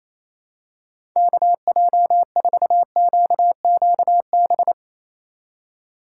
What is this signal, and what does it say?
Morse code, heard as a keyed tone, about 28 wpm, 700 Hz. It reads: KJ4QQB
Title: USA Call Signs - 6 Characters - 4x Character Spacing - 28wpm